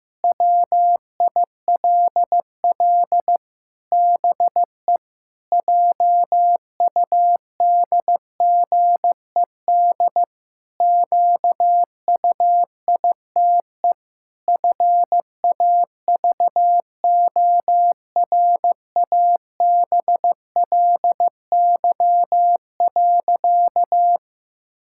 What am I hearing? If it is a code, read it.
Morse code, heard as a keyed tone, about 15 wpm, 700 Hz. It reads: WILL BE JUDGED QUITE FAVORABLY.
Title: Pangram Challenge v2 - 15wpm